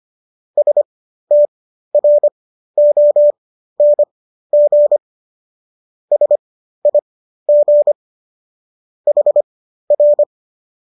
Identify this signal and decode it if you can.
Morse code, heard as a keyed tone, about 25 wpm, spaced as for 13 wpm, 590 Hz. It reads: STRONG SIG HR